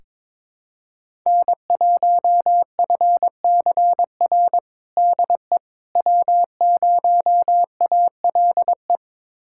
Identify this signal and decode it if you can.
Morse code, heard as a keyed tone, about 22 wpm, 700 Hz. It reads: N1FCR DE W0ALE
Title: USA Callsign DE Callsign - 5 Characters - 22wpm